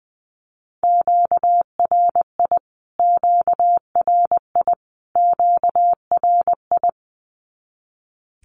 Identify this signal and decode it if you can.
Morse code, heard as a keyed tone, about 20 wpm, 700 Hz. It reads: QRI QRI QRI
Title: Q-Codes Repeated 3x 20wpm